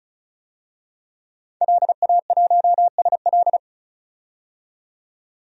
Morse code, heard as a keyed tone, about 34 wpm, 700 Hz. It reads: LA1SL
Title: DXCC-100 Call Signs - 5 Characters - 35wpm